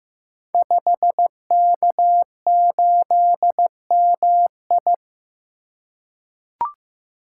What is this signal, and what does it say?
Morse code, heard as a keyed tone, about 15 wpm, 700 Hz. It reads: 5K8MI
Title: Code Groups - 5 Letter:Numbers - 15wpm